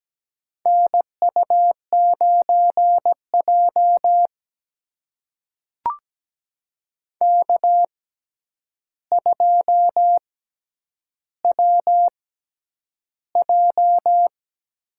Morse code, heard as a keyed tone, about 17 wpm, spaced as for 9 wpm, 700 Hz. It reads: NU9J K2WJ
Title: USA Call Signs - 4 Characters - 6x Character Spacing - 17wpm